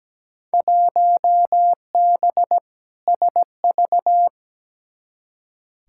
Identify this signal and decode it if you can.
Morse code, heard as a keyed tone, about 17 wpm, 700 Hz. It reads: 1B SV
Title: Field Day Exchanges Encore - 17wpm